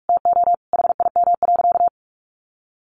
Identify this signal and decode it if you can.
Morse code, heard as a keyed tone, about 44 wpm, 700 Hz. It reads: TO HIM.
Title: Sentences from Top 1200 Words 45wpm